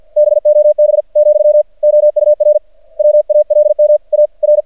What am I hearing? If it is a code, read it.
Morse code, heard as a keyed tone, about 54 wpm, spaced as for 45 wpm, 600 Hz. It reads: BOB , OUR WALMAR
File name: archived / CRSnet